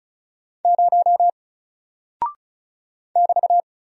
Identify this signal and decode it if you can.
Morse code, heard as a keyed tone, about 36 wpm, 700 Hz. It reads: 0 <BT>
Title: Single Character (Letter:Number:Punctuation:Prosign) 35wpm